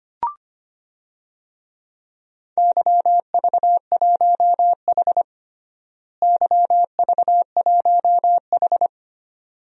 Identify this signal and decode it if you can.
Morse code, heard as a keyed tone, about 25 wpm, 700 Hz. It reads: YV1H YV1H
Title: DXCC-100 Call Signs - 4 Characters - Repeated 2x - 25wpm